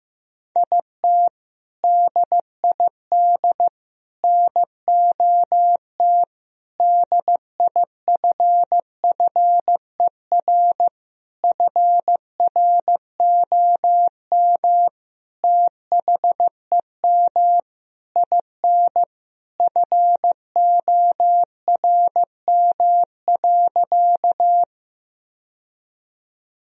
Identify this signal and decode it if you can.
Morse code, heard as a keyed tone, about 15 wpm, 700 Hz. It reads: IT DID NOT DIFFER FROM THEM IN FORM.
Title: Sentences from Top 1000 Words - Review Part 1 - 15wpm